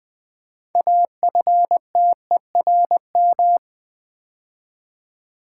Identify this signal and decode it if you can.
Morse code, heard as a keyed tone, about 20 wpm, 700 Hz. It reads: AFTERM
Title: Build A Word - 3 Letters At A Time - 9-Letter Words - 20wpm